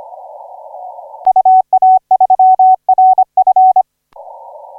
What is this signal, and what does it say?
Morse code, heard as a keyed tone, about 26 wpm, 750 Hz. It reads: UA3RF
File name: 8Q7CF 1817UTC 17m